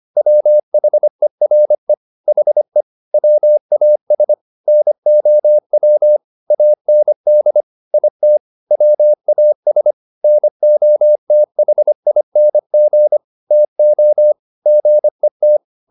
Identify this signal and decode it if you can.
Morse code, heard as a keyed tone, about 25 wpm, 600 Hz. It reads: WHERE HE WAS NOW AND IT WAS NOTHING TO GET